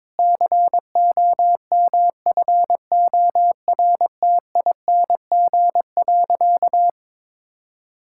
Morse code, heard as a keyed tone, about 22 wpm, 700 Hz. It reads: COMFORTING.
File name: Difficult Sentences with Hints - 22wpm